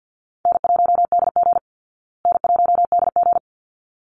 Morse code, heard as a keyed tone, about 50 wpm, 700 Hz. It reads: N1DG N1DG
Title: Easy Call Signs Repeated 2x - 50wpm